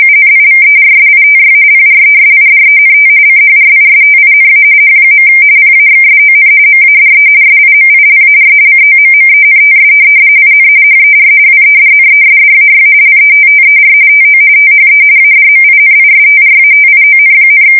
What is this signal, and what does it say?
RTTY, about 45.45 baud. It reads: UNIVERSITY PRESIDENT CHRISTINA H. PAXSON MADE SIMILAR COMMENTS IN HER WEDNESDAY LETTER TO THE ADMINISTRATION,
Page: 170 Shift 2125/2295